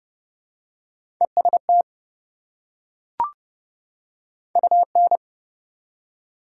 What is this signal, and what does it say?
Morse code, heard as a keyed tone, about 29 wpm, 700 Hz. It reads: EST UN
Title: Prefixes and Suffixes 30wpm